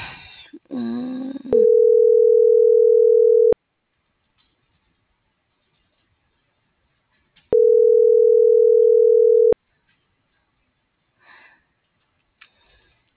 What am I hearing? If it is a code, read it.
Ringback tone (North America), 440 + 480 Hz.